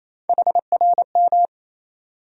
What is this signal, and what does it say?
Morse code, heard as a keyed tone, about 28 wpm, 700 Hz. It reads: HRM
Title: USA Call Sign Postfixes - 3 Letters 28wpm